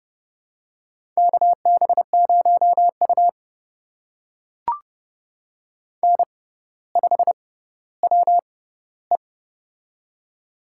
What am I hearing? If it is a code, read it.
Morse code, heard as a keyed tone, about 30 wpm, spaced as for 15 wpm, 700 Hz. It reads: KB0U N5WE
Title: USA Call Signs - 4 Characters - 6x Character Spacing - 30wpm